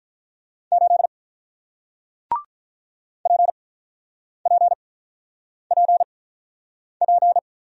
Morse code, heard as a keyed tone, about 48 wpm, spaced as for 9 wpm, 700 Hz. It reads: 8 PPPP